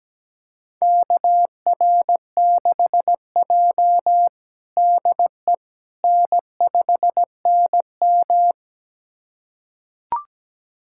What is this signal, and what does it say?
Morse code, heard as a keyed tone, about 17 wpm, 700 Hz. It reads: KR6J DE N5NM
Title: USA Callsign DE Callsign - 4 Characters - 17wpm